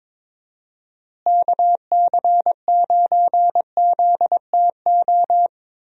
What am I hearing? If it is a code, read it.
Morse code, heard as a keyed tone, about 22 wpm, 700 Hz. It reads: KC9ZTO
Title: USA Call Signs - 6 Characters - Repeated 3x - 22wpm